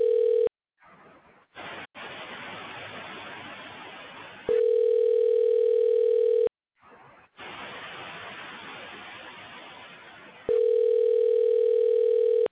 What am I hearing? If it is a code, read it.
Ringback tone (North America), 440 + 480 Hz.